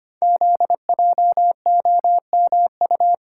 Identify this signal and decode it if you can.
Morse code, heard as a keyed tone, about 25 wpm, 700 Hz. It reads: ZJOMU
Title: Code Groups - 5 Letters - 25wpm